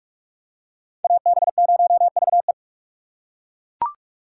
Morse code, heard as a keyed tone, about 39 wpm, 700 Hz. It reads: AB0VE